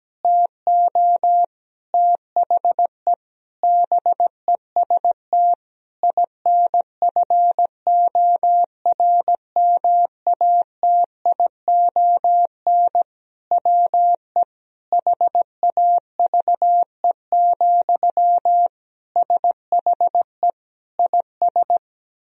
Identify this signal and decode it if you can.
Morse code, heard as a keyed tone, about 17 wpm, 700 Hz. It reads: TO THE BEST INFORMATION WE HAVE, SHE IS
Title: Sentences from Top 1000 Words - Review Part 4 - 17wpm